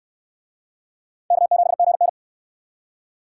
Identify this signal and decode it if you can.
Morse code, heard as a keyed tone, about 60 wpm, 700 Hz. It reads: K7GN